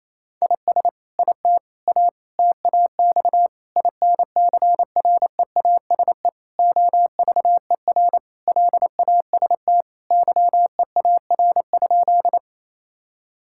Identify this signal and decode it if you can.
Morse code, heard as a keyed tone, about 28 wpm, 700 Hz. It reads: IS IT A TAX INCREASE OVER LAST YEAR?
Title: Sentences from Top 1000 Words - Review Part 5 - 28wpm